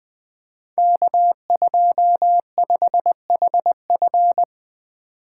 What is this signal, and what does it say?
Morse code, heard as a keyed tone, about 20 wpm, 700 Hz. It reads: K25HF
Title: Code Groups - 5 Letter:Numbers - 20wpm